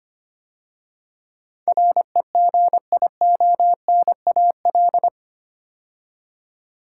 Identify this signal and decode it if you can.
Morse code, heard as a keyed tone, about 25 wpm, 700 Hz. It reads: REGIONAL
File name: 8 Letter Words - First 3 Letters Given - 25wpm